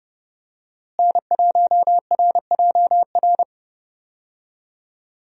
Morse code, heard as a keyed tone, about 32 wpm, 700 Hz. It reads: N1RJR
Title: USA Call Signs - 5 Characters - Repeated 3x - 30wpm